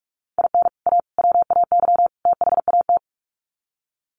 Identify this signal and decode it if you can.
Morse code, heard as a keyed tone, about 44 wpm, 700 Hz. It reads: IN A WAY THAT
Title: Sets of 4 Words 45wpm